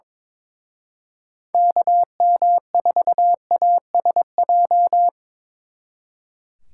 Morse code, heard as a keyed tone, about 26 wpm, 700 Hz. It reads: KM4ASJ
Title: Speed Racing - USA 2x3 Call Signs 22wpm